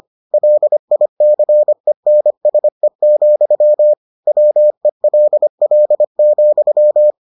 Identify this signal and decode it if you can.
Morse code, heard as a keyed tone, about 25 wpm, 600 Hz. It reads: LICENSE, WELL,